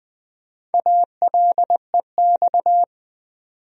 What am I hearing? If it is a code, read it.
Morse code, heard as a keyed tone, about 20 wpm, 700 Hz. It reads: ALEX
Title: USA Common Names - 20wpm